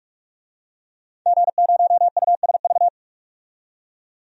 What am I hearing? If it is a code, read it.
Morse code, heard as a keyed tone, about 46 wpm, 700 Hz. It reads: G0USV
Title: DXCC-100 Call Signs - 5 Characters - 45wpm